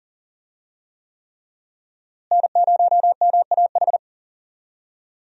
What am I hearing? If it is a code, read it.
Morse code, heard as a keyed tone, about 40 wpm, 690 Hz. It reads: N0MAH
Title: USA Call Signs - 5 Characters - Repeated 3x - 40wpm